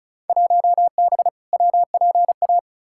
Morse code, heard as a keyed tone, about 36 wpm, 700 Hz. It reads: 1B WPA
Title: Field Day Exchanges - 35wpm